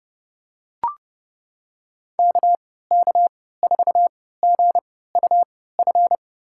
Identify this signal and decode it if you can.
Morse code, heard as a keyed tone, about 32 wpm, spaced as for 17 wpm, 690 Hz. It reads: KK4GUF
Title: USA Call Signs - 3x Character Spacing - 30wpm